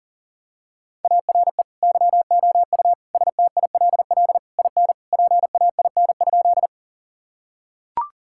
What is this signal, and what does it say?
Morse code, heard as a keyed tone, about 40 wpm, 700 Hz. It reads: ARE YOU STILL IN PAIN?
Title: Sentences from Top 1600 Words 40wpm